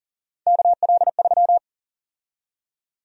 Morse code, heard as a keyed tone, about 40 wpm, 700 Hz. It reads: KL3
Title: USA Call Sign Prefixes - 40wpm